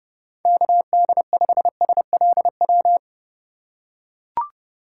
Morse code, heard as a keyed tone, about 32 wpm, 700 Hz. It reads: KD5SLW
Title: USA Call Signs - 6 Characters - 4x Character Spacing - 30wpm